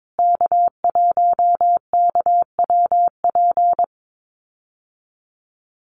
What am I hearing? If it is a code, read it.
Morse code, heard as a keyed tone, about 22 wpm, 700 Hz. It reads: K1KWP
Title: USA Call Signs Repeated 3x 22wpm